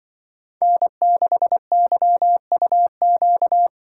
Morse code, heard as a keyed tone, about 23 wpm, 700 Hz. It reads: N6YUQ
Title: Speed Racing - USA 1x3 Call Signs 20wpm